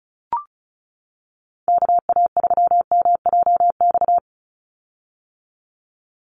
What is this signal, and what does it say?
Morse code, heard as a keyed tone, about 34 wpm, 700 Hz. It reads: KA3MJX
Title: USA Call Signs 35wpm